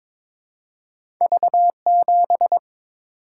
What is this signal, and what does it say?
Morse code, heard as a keyed tone, about 22 wpm, 700 Hz. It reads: V7
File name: Two Letter-Number Combinations - 22wpm